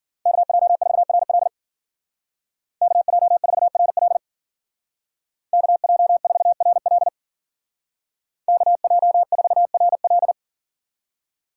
Morse code, heard as a keyed tone, about 48 wpm, 700 Hz. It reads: KJ4RL KJ4RL KJ4RL KJ4RL
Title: Speed Racing - USA 2x2 Call Signs 40wpm